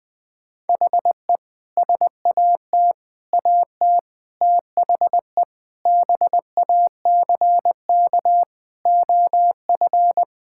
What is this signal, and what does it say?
Morse code, heard as a keyed tone, about 20 wpm, 700 Hz. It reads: HE SAT AT THE BACK OF
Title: Sentences with 3-Word Phrases - 20wpm